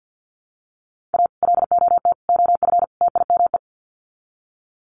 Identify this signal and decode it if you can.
Morse code, heard as a keyed tone, about 50 wpm, 700 Hz. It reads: A LOT OF TIME
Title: Sets of 4 Words 50wpm